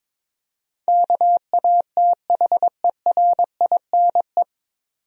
Morse code, heard as a keyed tone, about 22 wpm, 700 Hz. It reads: KATHERINE
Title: USA Common Names - 22wpm